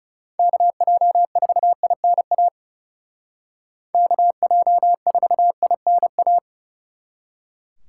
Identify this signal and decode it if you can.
Morse code, heard as a keyed tone, about 37 wpm, 700 Hz. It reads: KJ4INA KJ4INA
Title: Speed Racing - USA 2x3 Call Signs 30wpm